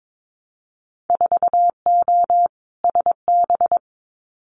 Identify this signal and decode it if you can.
Morse code, heard as a keyed tone, about 22 wpm, 700 Hz. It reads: 4O SB
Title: Winter Field Day Exchanges 22wpm